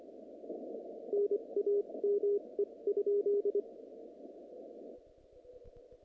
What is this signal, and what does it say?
Morse code, heard as a keyed tone, about 24 wpm, spaced as for 21 wpm, 410 Hz. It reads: NAME?
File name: contests / naqp-cw-k5tr-op-k5pi- -2200